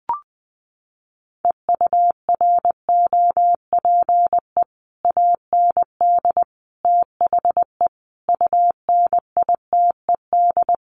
Morse code, heard as a keyed tone, about 20 wpm, 700 Hz. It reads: EUROPE AND THE UNITED
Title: Sets of 5 Words Encore2 20wpm